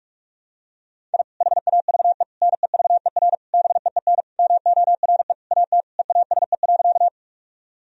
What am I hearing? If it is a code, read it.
Morse code, heard as a keyed tone, about 41 wpm, 700 Hz. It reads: I HAVE NEVER BEEN MORE AT EASE.
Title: Sentences from Top 1000 Words - Review Part 1 - 45wpm